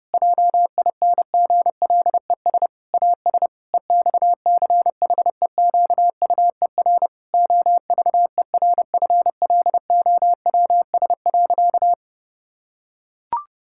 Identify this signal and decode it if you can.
Morse code, heard as a keyed tone, about 32 wpm, 700 Hz. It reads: JINGLES AS EXCHEQUER OVERFLOWS.
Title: Pangram Challenge v2 - 30wpm